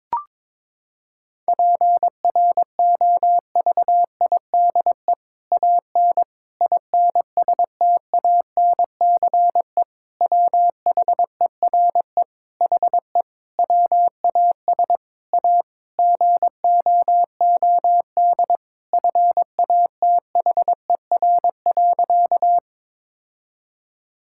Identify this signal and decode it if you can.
Morse code, heard as a keyed tone, about 22 wpm, 700 Hz. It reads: PROVIDE AN INSTANCE WHERE HE WAS A GOOD FATHER.
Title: Sentences from Top 1000 Words - Review Part 5 - 22wpm